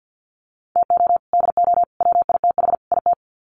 Morse code, heard as a keyed tone, about 50 wpm, 700 Hz. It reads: TO DO WITH IT
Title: Sets of 4 Words 50wpm